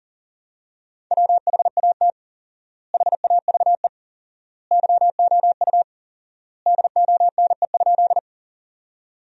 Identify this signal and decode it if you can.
Morse code, heard as a keyed tone, about 37 wpm, 700 Hz. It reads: WHAT HAVE YOU DONE?
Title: 4x Word-Spacing - Sentences from Top 500 Words - 40wpm